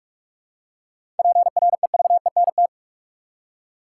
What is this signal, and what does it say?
Morse code, heard as a keyed tone, about 46 wpm, 700 Hz. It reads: PREVENT
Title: Seven Letter Words - 45wpm